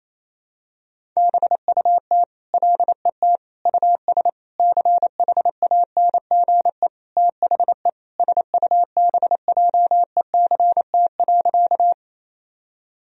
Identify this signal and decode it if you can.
Morse code, heard as a keyed tone, about 27 wpm, 700 Hz. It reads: BUT LET US CHANGE THE SUBJECT.
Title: Sentences from Top 1000 Words - Review Part 1 - 28wpm